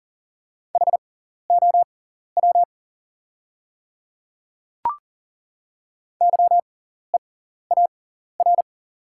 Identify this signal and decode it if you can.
Morse code, heard as a keyed tone, about 40 wpm, spaced as for 14 wpm, 700 Hz. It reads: HOW YEAR
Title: 6x Character Spacing - Top 200 Words - 40wpm